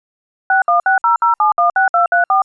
DTMF, heard as telephone touch tones, tones of about 150 ms. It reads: 61600716234